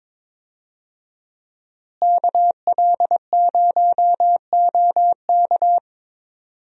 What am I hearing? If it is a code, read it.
Morse code, heard as a keyed tone, about 22 wpm, 700 Hz. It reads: KL0OK
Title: USA Call Signs - 5 Characters - Repeated 2x - 22wpm